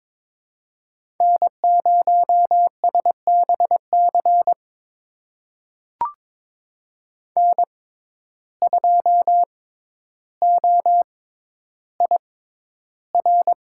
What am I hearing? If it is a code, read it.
Morse code, heard as a keyed tone, about 21 wpm, spaced as for 11 wpm, 700 Hz. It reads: N0SBC N2OIR
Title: USA Call Signs - 5 Characters - 6x Character Spacing - 22wpm